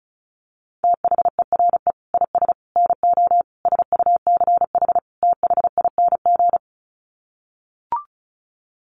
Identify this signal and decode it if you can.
Morse code, heard as a keyed tone, about 34 wpm, 700 Hz. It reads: THERE IS NO SUCH THING
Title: Sets of 5 Words Encore 35wpm